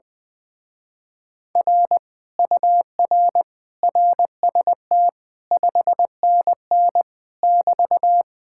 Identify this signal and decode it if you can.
Morse code, heard as a keyed tone, about 20 wpm, 700 Hz. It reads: R UR RST 5NN <BT>
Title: Speed-Racing Common QSO Phrases Encore - 20wpm